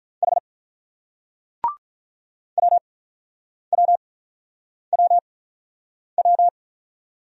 Morse code, heard as a keyed tone, about 47 wpm, spaced as for 9 wpm, 700 Hz. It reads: H WWWW